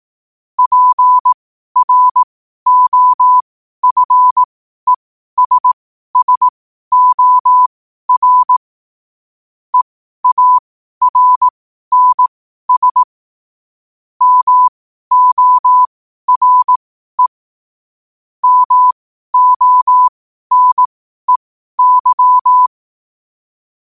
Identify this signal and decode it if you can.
Morse code, heard as a keyed tone, about 18 wpm, spaced as for 12 wpm, 1000 Hz. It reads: PROFESSOR EARNS MORE MONEY
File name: qotdfiles / 12wpm